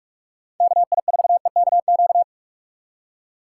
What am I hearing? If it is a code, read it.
Morse code, heard as a keyed tone, about 46 wpm, 700 Hz. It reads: KI4EKQ
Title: USA Call Signs - 6 Characters - 2x Character Spacing - 45wpm